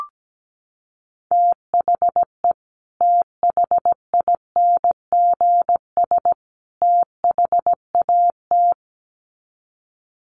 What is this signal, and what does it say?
Morse code, heard as a keyed tone, about 17 wpm, 700 Hz. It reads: THE THINGS THAT
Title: Sets of 3 Words 17wpm